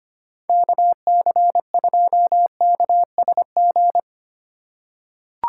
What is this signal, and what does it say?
Morse code, heard as a keyed tone, about 25 wpm, 700 Hz. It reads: KC2KSG
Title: USA Call Signs - 6 Characters - 2x Character Spacing - 25wpm